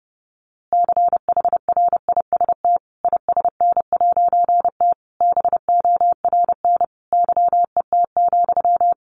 Morse code, heard as a keyed tone, about 30 wpm, 700 Hz. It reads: CHRIST ISN'T BORN YET,
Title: A Sound Of Thunder 30wpm